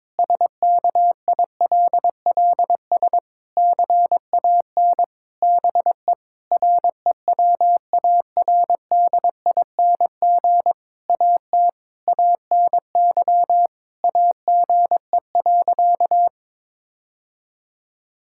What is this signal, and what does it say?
Morse code, heard as a keyed tone, about 22 wpm, 700 Hz. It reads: SKILLS CAN BE REWARDING AT ANY AGE.
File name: Difficult Sentences with Hints - 22wpm